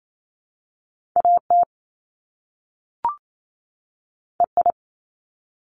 Morse code, heard as a keyed tone, about 28 wpm, 700 Hz. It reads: AT EI
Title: Top Two Letter Combinations in Words 28wpm